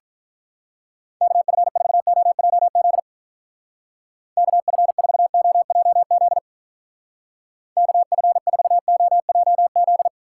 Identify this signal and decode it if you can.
Morse code, heard as a keyed tone, about 47 wpm, 700 Hz. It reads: KF4OJZ KF4OJZ KF4OJZ